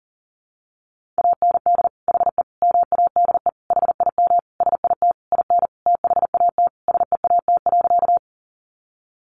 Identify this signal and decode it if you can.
Morse code, heard as a keyed tone, about 37 wpm, 700 Hz. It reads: AND HE MADE HIM SIT IN THAT SEAT.
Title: Sentences from Top 1500 Words 40wpm